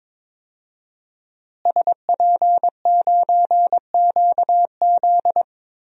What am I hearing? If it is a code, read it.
Morse code, heard as a keyed tone, about 21 wpm, 700 Hz. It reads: SP9QZ
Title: DXCC-100 Call Signs - 5 Characters - Repeated 3x - 22wpm